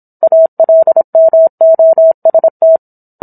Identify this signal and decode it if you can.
Morse code, heard as a keyed tone, about 27 wpm, 650 Hz. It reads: ALMOST
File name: Book One / 17-Chapter 17 - 26 WPM